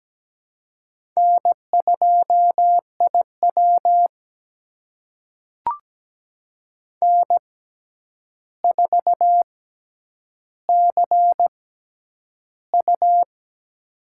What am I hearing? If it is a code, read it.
Morse code, heard as a keyed tone, about 17 wpm, spaced as for 9 wpm, 700 Hz. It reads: N2IW N4CU
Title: USA Call Signs - 4 Characters - 6x Character Spacing - 17wpm